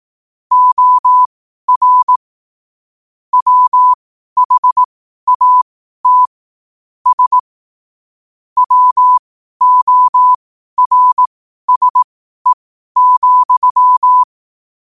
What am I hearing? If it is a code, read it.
Morse code, heard as a keyed tone, about 18 wpm, spaced as for 12 wpm, 1000 Hz. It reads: OR WHAT S WORSE,
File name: qotdfiles / 12wpm